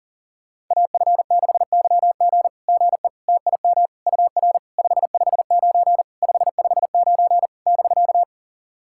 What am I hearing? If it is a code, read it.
Morse code, heard as a keyed tone, about 37 wpm, 690 Hz. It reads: AF6YG GE TIM UR 559 559 <BK>
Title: SOTA QSO Practice 40wpm